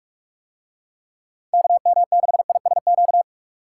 Morse code, heard as a keyed tone, about 46 wpm, 700 Hz. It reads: KM6ISQ